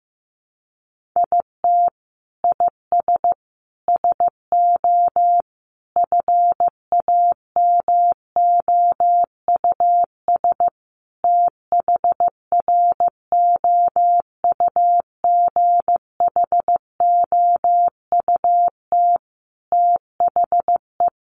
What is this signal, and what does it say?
Morse code, heard as a keyed tone, about 15 wpm, 700 Hz. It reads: IT IS SO FAMOUS THROUGHOUT THE
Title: Sentences from Top 1500 Words 15wpm